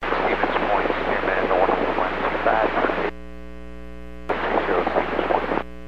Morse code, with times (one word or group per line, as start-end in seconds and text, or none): none